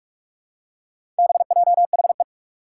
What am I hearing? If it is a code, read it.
Morse code, heard as a keyed tone, about 44 wpm, 700 Hz. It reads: BJHE
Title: Four Challenging Letter-Number Combinations 45wpm